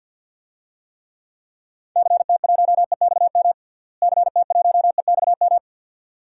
Morse code, heard as a keyed tone, about 50 wpm, 700 Hz. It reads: CT1EXM CT1EXM
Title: DXCC-100 Call Signs - 6 Characters - Repeated 2x - 50wpm